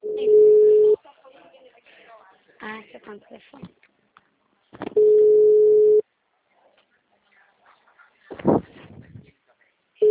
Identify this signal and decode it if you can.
Ringback tone (Europe, 425 Hz), with speech.